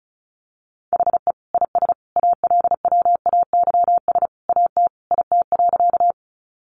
Morse code, heard as a keyed tone, about 34 wpm, 700 Hz. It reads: HE IS ALWAYS AT IT.
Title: Sentences from Top 300 Words 35wpm